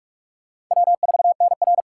Morse code, heard as a keyed tone, about 48 wpm, 700 Hz. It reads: W4NR
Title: USA Call Signs - 4 Characters - 2x Character Spacing - 45wpm